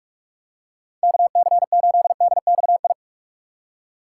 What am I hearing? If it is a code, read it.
Morse code, heard as a keyed tone, about 46 wpm, 700 Hz. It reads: KC8DXI